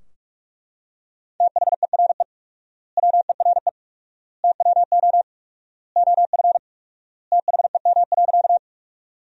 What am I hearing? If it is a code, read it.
Morse code, heard as a keyed tone, about 42 wpm, 700 Hz. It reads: THERE WERE TWO OF THEM.
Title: 4x Word-Spacing - Sentences from Top 100 Words - 45wpm